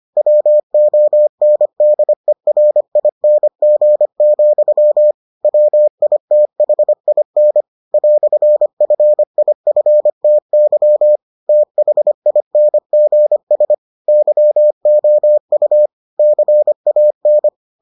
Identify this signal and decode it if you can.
Morse code, heard as a keyed tone, about 25 wpm, 600 Hz. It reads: WONDERING, WITHIN "FIFTY THINGS YOU CAN